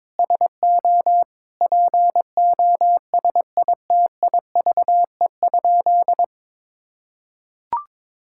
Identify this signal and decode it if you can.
Morse code, heard as a keyed tone, about 22 wpm, 700 Hz. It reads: SO POSITIVE?
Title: Sentences from Top 1000 Words - Review Part 5 - 22wpm